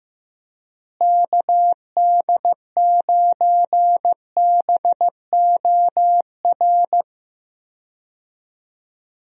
Morse code, heard as a keyed tone, about 15 wpm, 700 Hz. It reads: KD9BOR
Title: USA Call Signs - 6 Characters - Repeated 3x - 15wpm